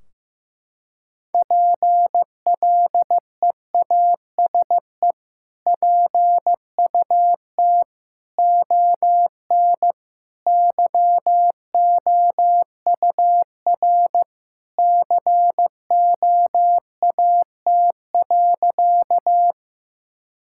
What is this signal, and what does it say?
Morse code, heard as a keyed tone, about 15 wpm, 700 Hz. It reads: PLEASE PUT ON YOUR COAT.
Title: Sentences from Top 1600 Words 15wpm